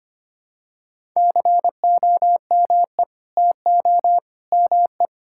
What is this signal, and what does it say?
Morse code, heard as a keyed tone, about 25 wpm, 700 Hz. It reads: COME TO ME
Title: Sentences from Top 1000 Words - Review Part 6 - 25wpm